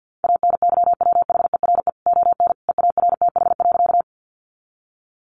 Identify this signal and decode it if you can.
Morse code, heard as a keyed tone, about 50 wpm, 700 Hz. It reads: ANYWHERE ON EARTH.
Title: Sentences from Top 1400 Words 50wpm